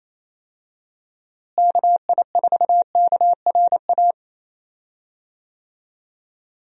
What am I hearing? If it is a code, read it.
Morse code, heard as a keyed tone, about 27 wpm, 700 Hz. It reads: KI4KRA
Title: USA Call Signs - 6 Characters - Repeated 3x - 28wpm